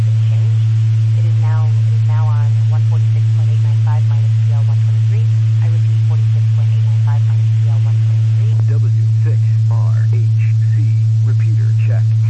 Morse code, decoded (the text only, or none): none